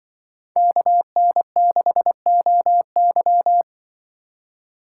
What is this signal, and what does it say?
Morse code, heard as a keyed tone, about 24 wpm, 700 Hz. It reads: KN6OY
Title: Speed Racing - USA 2x2 Call Signs 20wpm